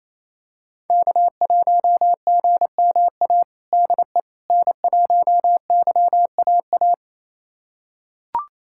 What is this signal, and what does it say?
Morse code, heard as a keyed tone, about 28 wpm, 700 Hz. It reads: K1GMA DE N1YAA
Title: USA Callsign DE Callsign - 5 Characters - 28wpm